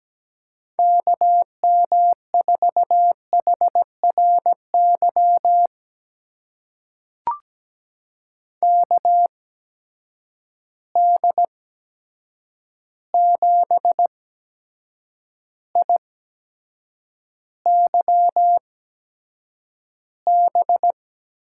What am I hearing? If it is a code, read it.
Morse code, heard as a keyed tone, about 17 wpm, spaced as for 7 wpm, 700 Hz. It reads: KM4HRY KD7IYB
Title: USA Call Signs - 6 Characters - 8x Character Spacing - 17wpm